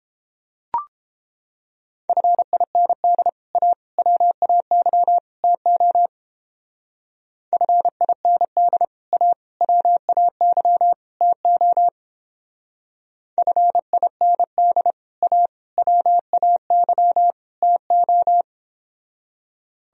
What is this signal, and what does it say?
Morse code, heard as a keyed tone, about 29 wpm, 700 Hz. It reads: FIND A WAY TO FIND A WAY TO FIND A WAY TO